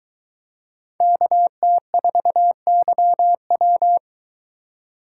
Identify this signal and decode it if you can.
Morse code, heard as a keyed tone, about 23 wpm, 700 Hz. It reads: KT4YW